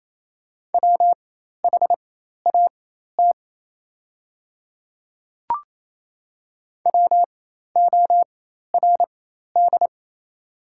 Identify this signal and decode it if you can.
Morse code, heard as a keyed tone, about 28 wpm, spaced as for 13 wpm, 700 Hz. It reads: WHAT WORD
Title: 4x Character Spacing - Top 100 Words - 28wpm